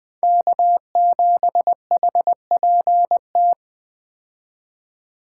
Morse code, heard as a keyed tone, about 20 wpm, 700 Hz. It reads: K7HPT
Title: K1USN SST QSO Practice - 20wpm